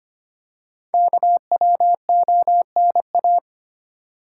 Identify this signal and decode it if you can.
Morse code, heard as a keyed tone, about 25 wpm, 700 Hz. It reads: KWONA
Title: Code Groups - 5 Letters - 25wpm